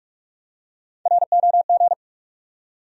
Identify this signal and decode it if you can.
Morse code, heard as a keyed tone, about 41 wpm, 700 Hz. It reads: ROG
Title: USA Common Names - Max 3 Letters - 45wpm